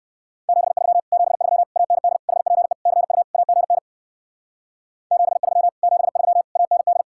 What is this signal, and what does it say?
Morse code, heard as a keyed tone, about 60 wpm, 700 Hz. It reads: 73 73 AND HPE CU AGN 73 73 AND